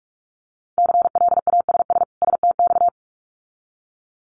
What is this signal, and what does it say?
Morse code, heard as a keyed tone, about 46 wpm, 700 Hz. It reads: CLASS STX
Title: Winter Field Day Exchanges 45wpm